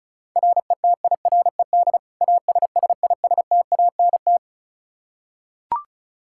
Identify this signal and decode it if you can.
Morse code, heard as a keyed tone, about 36 wpm, 700 Hz. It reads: RETIRED ASSISTANT
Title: Practice QSO - Exchange 3 - More Info - 35wpm